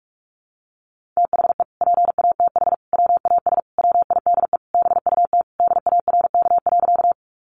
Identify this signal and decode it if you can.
Morse code, heard as a keyed tone, about 44 wpm, 700 Hz. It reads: THE PATH WAS WIDE BUT DARK.
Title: Sentences from Top 1400 Words 45wpm